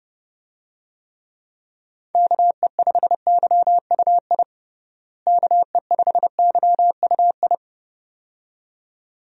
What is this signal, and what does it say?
Morse code, heard as a keyed tone, about 30 wpm, 700 Hz. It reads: KE5YUI KE5YUI
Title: USA Call Signs - 6 Characters - Repeated 2x - 30wpm